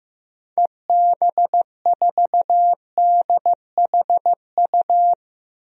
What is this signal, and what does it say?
Morse code, heard as a keyed tone, about 15 wpm, 700 Hz. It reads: EB4DHU
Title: DXCC-100 Call Signs - 6 Characters - Repeated 2x - 15wpm